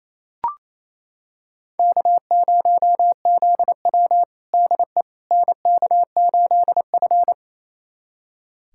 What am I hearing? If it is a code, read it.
Morse code, heard as a keyed tone, about 28 wpm, 700 Hz. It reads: K0ZW DE NK8F
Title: USA Callsign DE Callsign - 4 Characters - 28wpm